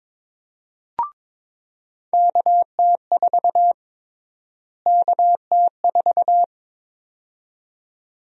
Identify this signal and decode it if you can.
Morse code, heard as a keyed tone, about 22 wpm, 700 Hz. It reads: KT4 KT4
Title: USA Call Sign Prefixes - Repeated 2x - 22wpm